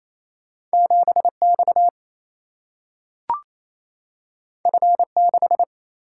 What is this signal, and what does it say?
Morse code, heard as a keyed tone, about 28 wpm, 700 Hz. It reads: 7 X F 6
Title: Single-Letter Single-Number 28wpm